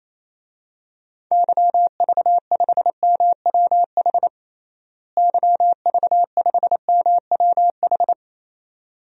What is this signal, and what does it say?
Morse code, heard as a keyed tone, about 27 wpm, 700 Hz. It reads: YV5MWH YV5MWH
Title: DXCC-100 Call Signs - 6 Characters - Repeated 2x - 28wpm